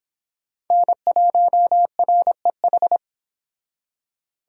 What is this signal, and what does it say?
Morse code, heard as a keyed tone, about 25 wpm, 700 Hz. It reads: N1REH
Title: Speed Racing - USA 1x3 Call Signs 22wpm